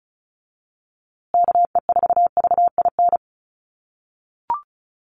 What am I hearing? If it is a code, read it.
Morse code, heard as a keyed tone, about 34 wpm, 700 Hz. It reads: KE4VIN
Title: USA Call Signs Repeated 3x 35wpm